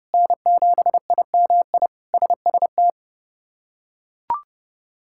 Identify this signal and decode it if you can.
Morse code, heard as a keyed tone, about 30 wpm, 700 Hz. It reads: N7IMI SST
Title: K1USN SST QSO Practice - 30wpm